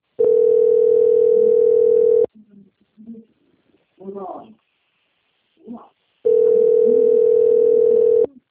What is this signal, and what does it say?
Ringback tone (North America), 440 + 480 Hz, with speech.
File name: out- - - - - .14876